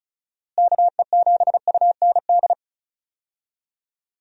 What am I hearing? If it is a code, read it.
Morse code, heard as a keyed tone, about 36 wpm, 700 Hz. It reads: KE7UND
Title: USA Call Signs - 6 Characters - Repeated 3x - 35wpm